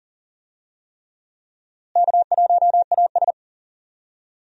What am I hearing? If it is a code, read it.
Morse code, heard as a keyed tone, about 40 wpm, 700 Hz. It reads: K1AS
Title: USA Call Signs - 4 Characters - Repeated 3x - 40wpm